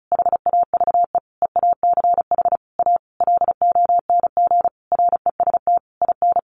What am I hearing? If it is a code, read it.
Morse code, heard as a keyed tone, about 34 wpm, 700 Hz. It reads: HAVE EACH A LONG REST IN
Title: Sentences from Top 700 Words 35wpm